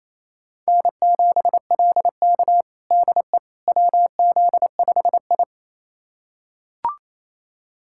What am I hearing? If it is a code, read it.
Morse code, heard as a keyed tone, about 28 wpm, 700 Hz. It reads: N7LK DE WZ5I
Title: USA Callsign DE Callsign - 4 Characters - 28wpm